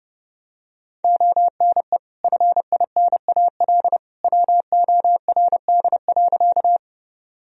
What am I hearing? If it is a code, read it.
Morse code, heard as a keyed tone, about 30 wpm, 700 Hz. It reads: ONE FINAL WORD.
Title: Sentences from Top 1000 Words - Review Part 2 - 30wpm